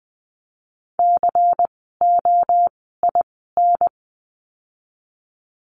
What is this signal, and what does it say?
Morse code, heard as a keyed tone, about 20 wpm, spaced as for 14 wpm, 700 Hz. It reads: COIN
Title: 094 Lesson 16f - Letter c - Words 20wpm